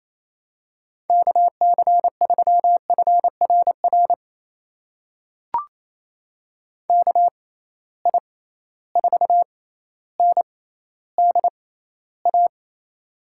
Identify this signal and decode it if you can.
Morse code, heard as a keyed tone, about 28 wpm, spaced as for 14 wpm, 700 Hz. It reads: KC3FRR KI4NDA
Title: USA Call Signs - 6 Characters - 6x Character Spacing - 28wpm